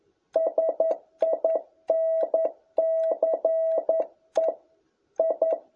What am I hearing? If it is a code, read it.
Morse code, heard as a keyed tone, about 11 wpm, 650 Hz. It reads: SINCE I
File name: jh8cbh / 2021cw